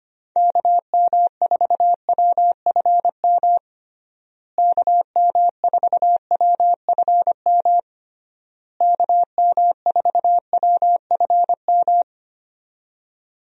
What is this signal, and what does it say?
Morse code, heard as a keyed tone, about 25 wpm, 700 Hz. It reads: KM4WFM KM4WFM KM4WFM
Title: USA Call Signs - 6 Characters - Repeated 3x - 25wpm